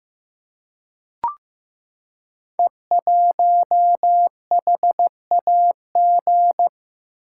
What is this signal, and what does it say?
Morse code, heard as a keyed tone, about 16 wpm, 700 Hz. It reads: E1HAG
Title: Code Groups - 5 Letter:Numbers - 15wpm